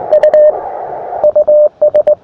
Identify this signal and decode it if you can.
Morse code, heard as a keyed tone, about 20 wpm, 590 Hz, with radio noise. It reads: U US
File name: attachments / us5wdv 20m